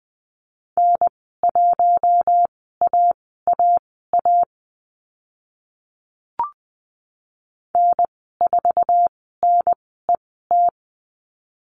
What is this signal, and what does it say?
Morse code, heard as a keyed tone, about 20 wpm, spaced as for 14 wpm, 700 Hz. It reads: N1AAA N4NET
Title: 062 Lesson 11g - Letter h - Call Signs 20wpm